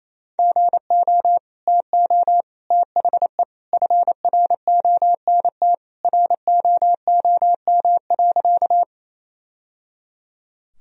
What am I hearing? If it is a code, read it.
Morse code, heard as a keyed tone, about 28 wpm, 700 Hz. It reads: GO TO THE FRONT ROOM.
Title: Sentences from Top 1000 Words - Review Part 2 - 28wpm